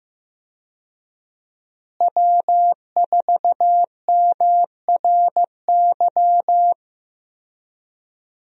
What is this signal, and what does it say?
Morse code, heard as a keyed tone, about 15 wpm, 700 Hz. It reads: W4MRY
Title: USA Call Signs - 5 Characters - Repeated 2x - 15wpm